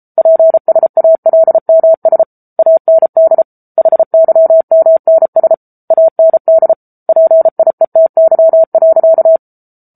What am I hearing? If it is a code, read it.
Morse code, heard as a keyed tone, about 33 wpm, 650 Hz. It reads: PSALMS AND HYMNS AND PIETY.
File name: Book Two / 07-Chapter 7 - 33 WPM